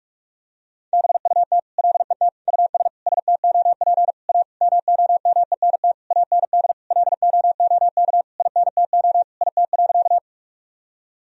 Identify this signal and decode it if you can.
Morse code, heard as a keyed tone, about 41 wpm, 700 Hz. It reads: BUT LET US STOP A MOMENT AND LOOK INTO IT.
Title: Sentences from Top 1000 Words - Review Part 1 - 45wpm